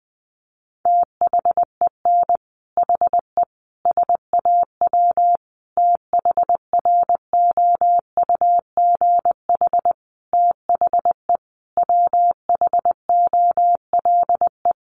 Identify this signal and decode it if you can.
Morse code, heard as a keyed tone, about 20 wpm, 700 Hz. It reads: THEN HE SAW THROUGH THE WHOLE
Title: Sentences from Top 400 Words 20wpm